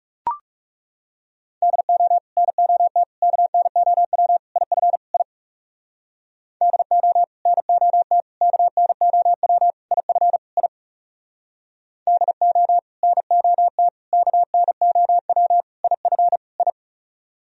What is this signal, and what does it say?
Morse code, heard as a keyed tone, about 37 wpm, 700 Hz. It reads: DO NOT KNOW IF I DO NOT KNOW IF I DO NOT KNOW IF I
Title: Speed-Racing - Sets of 5 Words 30wpm